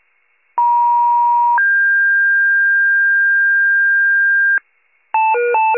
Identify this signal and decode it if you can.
Two-tone page, heard as a one-second tone, then a three-second tone, about 950 then 1600 Hz, with noise. It is heard from 0.6 to 4.6 s.
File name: wcalert